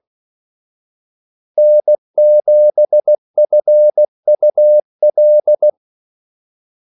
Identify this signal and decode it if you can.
Morse code, heard as a keyed tone, about 16 wpm, 600 Hz. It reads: N7FUL
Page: Kaldesignaler 16 wpm